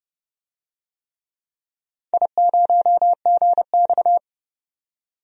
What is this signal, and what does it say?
Morse code, heard as a keyed tone, about 30 wpm, 700 Hz. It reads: I0GX
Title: DXCC-100 Call Signs - 4 Characters - 30wpm